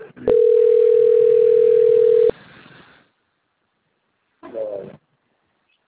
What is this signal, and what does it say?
Ringback tone (North America), 440 + 480 Hz.